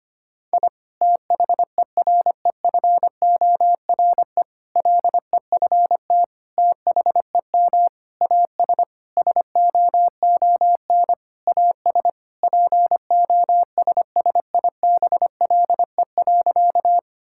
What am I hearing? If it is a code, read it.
Morse code, heard as a keyed tone, about 25 wpm, 700 Hz. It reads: I THEREFORE LEFT THEM AS SOON AS POSSIBLE.
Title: Sentences from Top 1000 Words - Review Part 5 - 25wpm